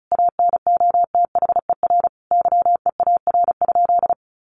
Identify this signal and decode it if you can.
Morse code, heard as a keyed tone, about 34 wpm, 700 Hz. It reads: ANOTHER YEAR?
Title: Sentences from Top 1500 Words 35wpm